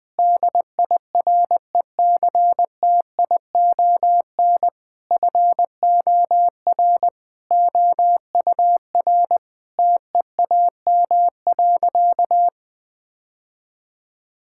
Morse code, heard as a keyed tone, about 20 wpm, 700 Hz. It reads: DIRECTION FOR OUR TEAM.
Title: Sentences with 8-Word Phrases - 20wpm